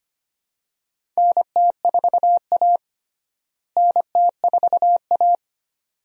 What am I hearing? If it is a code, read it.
Morse code, heard as a keyed tone, about 25 wpm, 700 Hz. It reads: NT4A NT4A
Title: USA Call Signs - 4 Characters - Repeated 2x - 25wpm